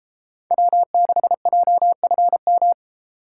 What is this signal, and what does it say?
Morse code, heard as a keyed tone, about 33 wpm, 700 Hz. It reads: W6JFM